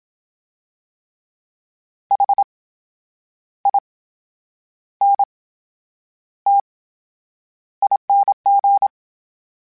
Morse code, heard as a keyed tone, about 27 wpm, spaced as for 9 wpm, 800 Hz. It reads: HINTING
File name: Special INGs / Special INGs-mp3s / VerySlowToFaster Ing @25